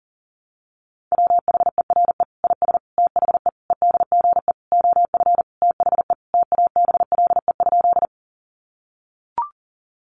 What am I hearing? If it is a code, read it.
Morse code, heard as a keyed tone, about 40 wpm, 700 Hz. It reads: WHERE IS THE EDGE OF THE TABLE?
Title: Sentences from Top 1300 Words 40wpm